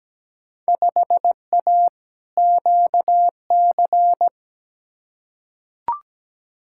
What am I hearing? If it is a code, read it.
Morse code, heard as a keyed tone, about 17 wpm, 700 Hz. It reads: 5A QC
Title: Field Day Exchanges Encore - 17wpm